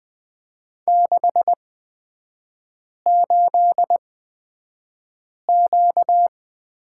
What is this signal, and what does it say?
Morse code, heard as a keyed tone, about 20 wpm, spaced as for 5 wpm, 700 Hz. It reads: 68Q
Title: 236 Lesson 38e - Letter q - Three Characters 20wpm